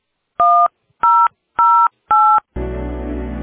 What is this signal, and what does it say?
DTMF, heard as telephone touch tones, tones of about 300 ms. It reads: 1008